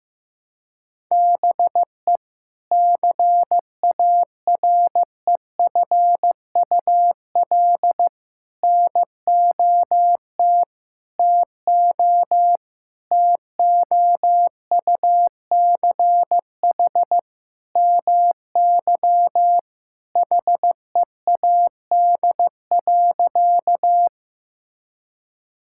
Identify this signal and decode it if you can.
Morse code, heard as a keyed tone, about 15 wpm, 700 Hz. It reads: BE CAREFUL NOT TO TOUCH MY HEAD.
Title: Sentences from Top 1600 Words 15wpm